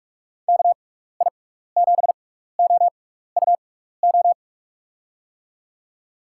Morse code, heard as a keyed tone, about 44 wpm, spaced as for 15 wpm, 700 Hz. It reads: KI7OUO
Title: USA Call Signs - 6x Character Spacing - 45wpm